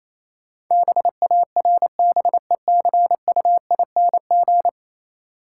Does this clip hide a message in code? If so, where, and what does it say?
Yes: Morse code from 0.7 to 4.7 s, heard as a keyed tone, about 28 wpm, 690 Hz. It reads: BARBECUING